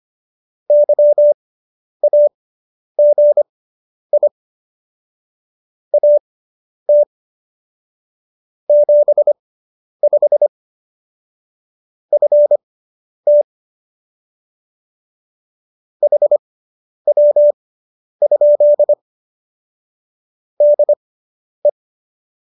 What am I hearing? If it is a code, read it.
Morse code, heard as a keyed tone, about 25 wpm, spaced as for 10 wpm, 590 Hz. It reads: YAGI AT 75 FT HW? DE
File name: QSO220 10